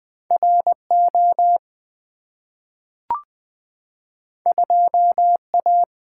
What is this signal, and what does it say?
Morse code, heard as a keyed tone, about 20 wpm, 700 Hz. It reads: RO 2A
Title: Two Letter-Number Combinations - 20wpm